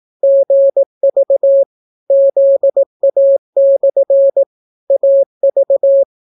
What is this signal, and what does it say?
Morse code, heard as a keyed tone, about 18 wpm, 550 Hz. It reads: GV ZA/ AV